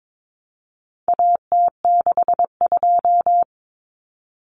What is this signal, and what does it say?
Morse code, heard as a keyed tone, about 22 wpm, 700 Hz. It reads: AT62
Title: Common Cut Numbers - 4-Digit 22wpm